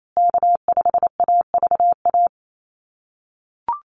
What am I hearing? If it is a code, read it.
Morse code, heard as a keyed tone, about 28 wpm, 700 Hz. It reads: K5AVA
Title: USA Call Signs Repeated 3x 28wpm